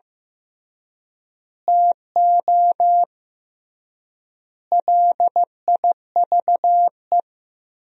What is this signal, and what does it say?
Morse code, heard as a keyed tone, about 15 wpm, 700 Hz. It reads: TO LIVE
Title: 3x Word-Spacing - Sentences from Top 200 Words - 15wpm